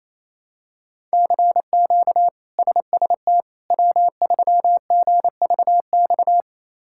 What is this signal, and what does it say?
Morse code, heard as a keyed tone, about 27 wpm, 700 Hz. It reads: CQ SST W3GVX
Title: K1USN SST QSO Practice - 28wpm